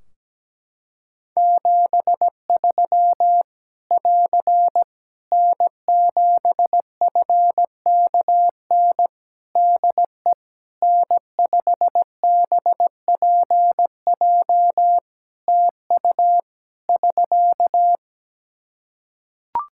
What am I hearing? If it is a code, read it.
Morse code, heard as a keyed tone, about 17 wpm, 700 Hz. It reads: 73 <AR> N7FKN DE N5BPJ TU <SK>
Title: Practice QSO - Exchange 4 - Ending - 17wpm